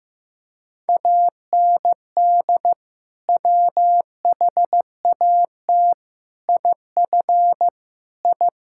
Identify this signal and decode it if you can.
Morse code, heard as a keyed tone, about 15 wpm, 700 Hz. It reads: AND WHAT IF I
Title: Sentences from Top 1000 Words - Review Part 5 - 15wpm